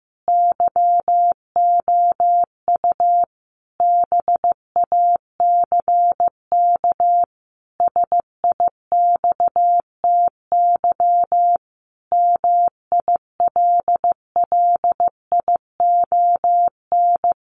Morse code, heard as a keyed tone, about 15 wpm, 700 Hz. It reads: YOU BACK SIXTY MILLION
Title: A Sound Of Thunder 15wpm